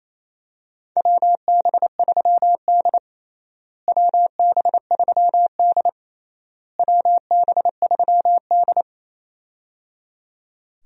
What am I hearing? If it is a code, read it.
Morse code, heard as a keyed tone, about 28 wpm, 700 Hz. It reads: WB3D WB3D WB3D
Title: USA Call Signs - 4 Characters - Repeated 3x - 28wpm